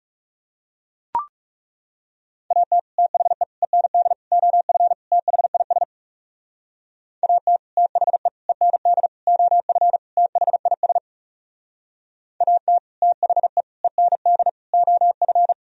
Morse code, heard as a keyed tone, about 40 wpm, 700 Hz. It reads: AT THE END OF THIS AT THE END OF THIS AT THE END OF